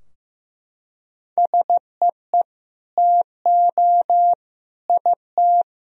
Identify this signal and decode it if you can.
Morse code, heard as a keyed tone, about 15 wpm, 700 Hz. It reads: SEE TO IT
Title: Sentences from Top 1000 Words - Review Part 2 - 15wpm